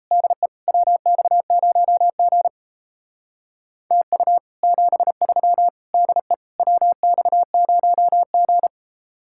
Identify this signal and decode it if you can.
Morse code, heard as a keyed tone, about 36 wpm, 700 Hz. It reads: DE WX0G TU 73 DE WX0G